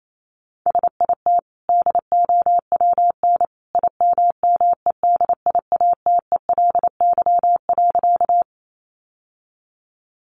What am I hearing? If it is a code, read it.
Morse code, heard as a keyed tone, about 28 wpm, 700 Hz. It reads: SIT DOWN IMMEDIATELY.
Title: Sentences from Top 1100 Words 28wpm